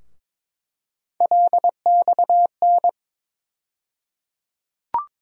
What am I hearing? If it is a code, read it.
Morse code, heard as a keyed tone, about 22 wpm, 700 Hz. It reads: LXN
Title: USA Call Sign Postfixes - 3 Letters 22wpm